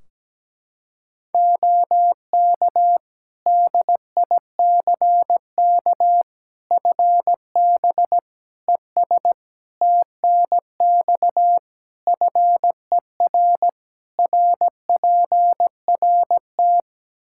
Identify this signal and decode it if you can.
Morse code, heard as a keyed tone, about 17 wpm, 700 Hz. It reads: OK DICK FB ES TNX FER RPRT
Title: Practice QSO - Exchange 2 - Info - 17wpm